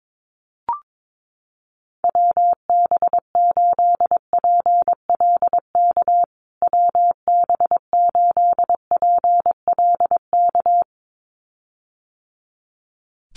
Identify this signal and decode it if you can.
Morse code, heard as a keyed tone, about 22 wpm, 700 Hz. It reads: WB8PLK WB8PLK
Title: USA Call Signs Repeated 2x 22wpm